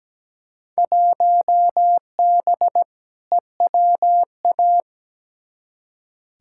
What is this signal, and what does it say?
Morse code, heard as a keyed tone, about 17 wpm, 700 Hz. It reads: 1B EWA
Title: Field Day Exchanges Encore - 17wpm